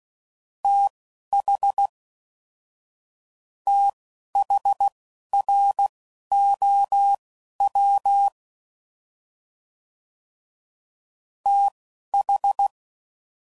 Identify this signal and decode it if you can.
Morse code, heard as a keyed tone, about 16 wpm, spaced as for 12 wpm, 800 Hz. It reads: TH THROW TH
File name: Prefixes / FLs & THs / TH Prefixes 1